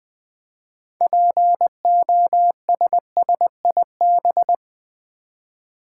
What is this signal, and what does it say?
Morse code, heard as a keyed tone, about 20 wpm, 700 Hz. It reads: POSSIB
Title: Build A Word - 2 Letters At A Time - 8-Letter Words - 20wpm